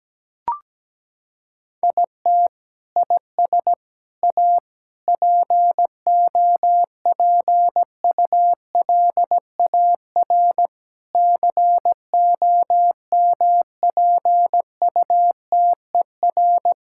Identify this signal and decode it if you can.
Morse code, heard as a keyed tone, about 17 wpm, 700 Hz. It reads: IT IS A POPULAR COMPUTER
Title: Sentences from Top 1000 Words - Review Part 6 - 17wpm